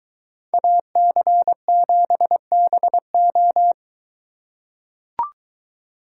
Morse code, heard as a keyed tone, about 22 wpm, 700 Hz. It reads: AC7BO